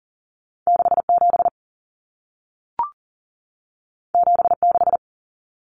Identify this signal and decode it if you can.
Morse code, heard as a keyed tone, about 40 wpm, 700 Hz. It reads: 67 76
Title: Common Cut Numbers - 2-Digit 40wpm